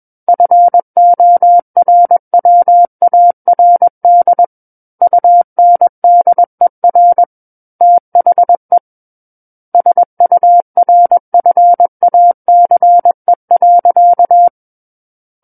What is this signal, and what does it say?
Morse code, heard as a keyed tone, about 21 wpm, 700 Hz. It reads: FORWARD UNDER THE SURFACE.
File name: Book One / 12-Chapter 12 - 21 WPM